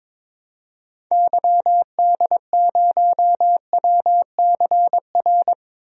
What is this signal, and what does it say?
Morse code, heard as a keyed tone, about 21 wpm, 700 Hz. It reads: YD0WCR
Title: DXCC-100 Call Signs - 6 Characters - Repeated 3x - 22wpm